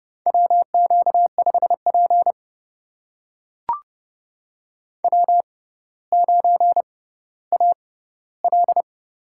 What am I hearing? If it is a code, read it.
Morse code, heard as a keyed tone, about 30 wpm, spaced as for 15 wpm, 700 Hz. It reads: WQ5P W9AL
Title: USA Call Signs - 4 Characters - 6x Character Spacing - 30wpm